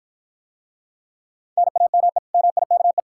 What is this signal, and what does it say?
Morse code, heard as a keyed tone, about 54 wpm, 700 Hz. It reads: NAME MIKE